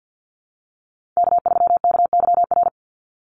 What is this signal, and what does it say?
Morse code, heard as a keyed tone, about 50 wpm, 700 Hz. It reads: K3KYR
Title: NA SOTA Chaser Callsigns 50wpm